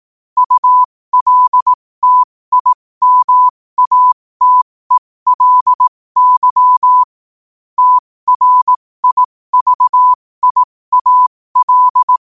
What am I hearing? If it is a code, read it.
Morse code, heard as a keyed tone, about 18 wpm, spaced as for 15 wpm, 1000 Hz. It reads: ULTIMATELY TRIVIAL
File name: qotdfiles / 15wpm